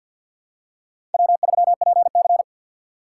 Morse code, heard as a keyed tone, about 46 wpm, 700 Hz. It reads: W3PC
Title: USA Call Signs - 4 Characters - 2x Character Spacing - 50wpm